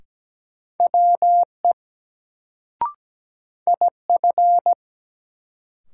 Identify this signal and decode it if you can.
Morse code, heard as a keyed tone, about 17 wpm, 700 Hz. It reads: WE IF
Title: Sets of 2 Words - New Word Drill - 17wpm